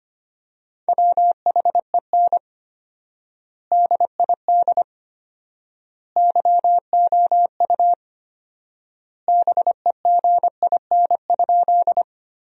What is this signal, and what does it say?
Morse code, heard as a keyed tone, about 25 wpm, 700 Hz. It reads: WHEN DID YOU BEGIN?
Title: 4x Word-Spacing - Sentences from Top 300 Words - 25wpm